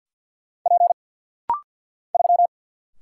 Morse code, heard as a keyed tone, about 50 wpm, 700 Hz. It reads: P 3
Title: Single Letter-Number Alt - Rapid-Fire 50wpm